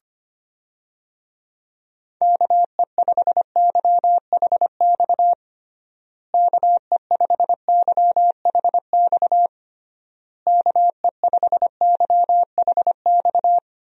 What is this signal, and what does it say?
Morse code, heard as a keyed tone, about 25 wpm, 700 Hz. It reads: KE5YHX KE5YHX KE5YHX
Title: USA Call Signs - 6 Characters - Repeated 3x - 25wpm